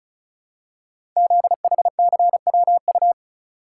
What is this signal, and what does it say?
Morse code, heard as a keyed tone, about 36 wpm, 700 Hz. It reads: ZHCWU
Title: Code Groups - 5 Letters - 35wpm